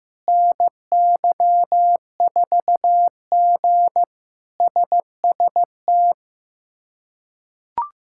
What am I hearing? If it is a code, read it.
Morse code, heard as a keyed tone, about 15 wpm, 700 Hz. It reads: NY4G SST
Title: K1USN SST QSO Practice - 15wpm